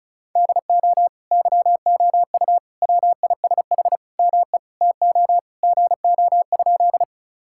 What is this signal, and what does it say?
Morse code, heard as a keyed tone, about 34 wpm, 700 Hz. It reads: DO YOU WISH ME TO GO?
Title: Sentences from Top 1600 Words 35wpm